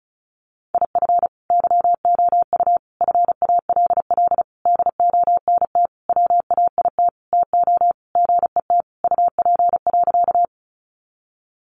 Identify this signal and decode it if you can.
Morse code, heard as a keyed tone, about 34 wpm, 700 Hz. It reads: IF YOU FALL DONT WAIT TO GET UP.
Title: Sentences from Top 500 Words 35wpm